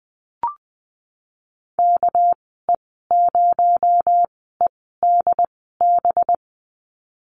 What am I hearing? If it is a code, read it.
Morse code, heard as a keyed tone, about 20 wpm, spaced as for 15 wpm, 700 Hz. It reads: KE0EDB
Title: 238 Lesson 38g - Letter q - Call Signs 20wpm